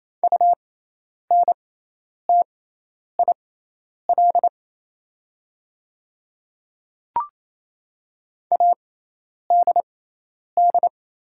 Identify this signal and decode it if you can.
Morse code, heard as a keyed tone, about 28 wpm, spaced as for 10 wpm, 690 Hz. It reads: UNTIL ADD
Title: 6x Character Spacing - Top 300 Words - 28wpm